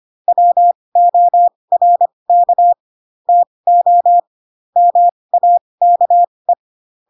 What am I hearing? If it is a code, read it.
Morse code, heard as a keyed tone, about 25 wpm, spaced as for 20 wpm, 700 Hz. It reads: WORK TO MAKE
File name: Pinocchio / Pinocchio ch3 15 25 ik1hge